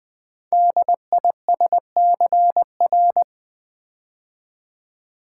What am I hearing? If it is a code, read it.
Morse code, heard as a keyed tone, about 20 wpm, 700 Hz. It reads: DISCR
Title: Build A Word - 5 Letters At A Time - 10-Letter Words - 20wpm